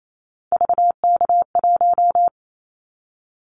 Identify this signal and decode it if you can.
Morse code, heard as a keyed tone, about 28 wpm, 700 Hz. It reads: VK1
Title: Callsign Prefixes for Top 50 DXCC - 28wpm